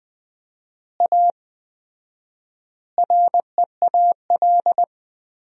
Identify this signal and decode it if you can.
Morse code, heard as a keyed tone, about 20 wpm, 700 Hz. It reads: A REAL
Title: 4x Word-Spacing - Sentences from Top 500 Words - 20wpm